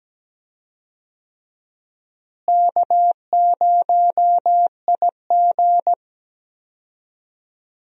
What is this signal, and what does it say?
Morse code, heard as a keyed tone, about 17 wpm, 700 Hz. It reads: K0IG
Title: USA Call Signs - 4 Characters - Repeated 2x - 17wpm